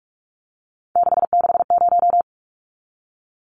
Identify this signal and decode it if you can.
Morse code, heard as a keyed tone, about 44 wpm, 700 Hz. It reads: 660
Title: Numbers - 3-Digit 45wpm